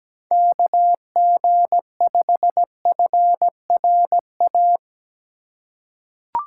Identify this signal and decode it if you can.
Morse code, heard as a keyed tone, about 17 wpm, 700 Hz. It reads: KG5FRA
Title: USA Call Signs - 6 Characters - 8x Character Spacing - 17wpm